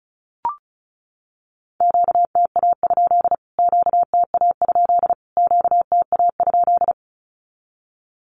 Morse code, heard as a keyed tone, about 34 wpm, 700 Hz. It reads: QTA? QTA? QTA?
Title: Q-Codes Repeated 3x 35wpm